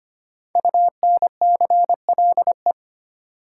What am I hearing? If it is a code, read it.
Morse code, heard as a keyed tone, about 25 wpm, 700 Hz. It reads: UNCLE
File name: 5 Letter Words - First 3 Letters Given - 25wpm